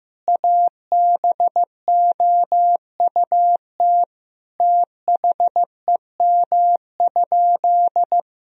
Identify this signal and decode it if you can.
Morse code, heard as a keyed tone, about 16 wpm, 700 Hz. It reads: ABOUT THEM?
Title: Sentences from Top 1000 Words - Review Part 1 - 15wpm